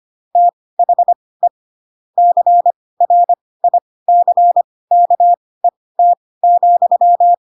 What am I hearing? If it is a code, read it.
Morse code, heard as a keyed tone, about 25 wpm, spaced as for 18 wpm, 700 Hz. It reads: THE CRICKET,
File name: Pinocchio / Pinocchio ch4 12 25 ik1hge